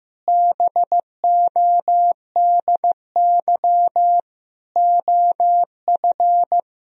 Morse code, heard as a keyed tone, about 15 wpm, 700 Hz. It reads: BODY OF
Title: Sentences from Top 1000 Words - Review Part 2 - 15wpm